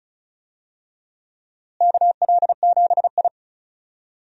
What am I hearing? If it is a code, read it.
Morse code, heard as a keyed tone, about 34 wpm, 700 Hz. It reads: KL7I
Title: DXCC-100 Call Signs - 4 Characters - Repeated 2x - 35wpm